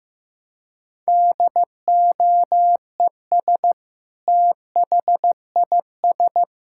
Morse code, heard as a keyed tone, about 15 wpm, 700 Hz. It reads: DOES THIS
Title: Sentences from Top 1000 Words - Review Part 5 - 15wpm